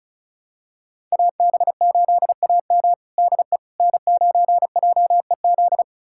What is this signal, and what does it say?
Morse code, heard as a keyed tone, about 36 wpm, 700 Hz. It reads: AB8AM DE N9JEZ
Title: USA Callsign DE Callsign - 5 Characters - 35wpm